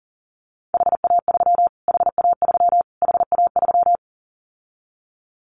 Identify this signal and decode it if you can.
Morse code, heard as a keyed tone, about 40 wpm, 700 Hz. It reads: HA3 HA3 HA3
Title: Callsign Prefixes for Top 25 DXCC - Repeated 3x - 40wpm